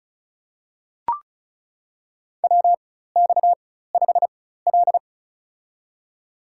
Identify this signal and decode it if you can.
Morse code, heard as a keyed tone, about 35 wpm, spaced as for 16 wpm, 700 Hz. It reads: WX5L
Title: USA Call Signs - 4 Characters - 4x Character Spacing - 35wpm